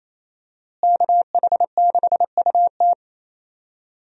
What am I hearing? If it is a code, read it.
Morse code, heard as a keyed tone, about 27 wpm, 700 Hz. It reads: KH6UT
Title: USA Call Signs - 5 Characters - Repeated 2x - 28wpm